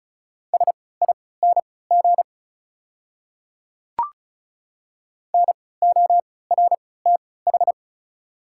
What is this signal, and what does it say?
Morse code, heard as a keyed tone, about 36 wpm, spaced as for 20 wpm, 700 Hz. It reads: SING NORTH
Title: 3x Character Spacing - Top 500 Words - 35wpm